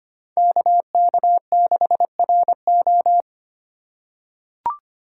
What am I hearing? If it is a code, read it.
Morse code, heard as a keyed tone, about 25 wpm, 700 Hz. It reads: KK6RO
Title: USA Call Signs - 5 Characters - 8x Character Spacing - 25wpm